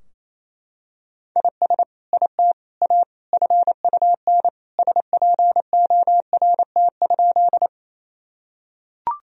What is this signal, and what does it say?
Morse code, heard as a keyed tone, about 28 wpm, 700 Hz. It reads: IS IT A FUN SPORT?
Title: Sentences from Top 1000 Words - Review Part 4 - 28wpm